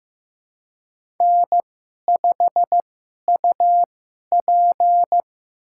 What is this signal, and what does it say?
Morse code, heard as a keyed tone, about 15 wpm, spaced as for 11 wpm, 700 Hz. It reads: N5UP
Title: USA Call Signs - 4 Characters - 2x Character Spacing - 15wpm